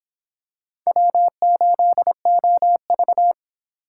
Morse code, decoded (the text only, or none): W8OV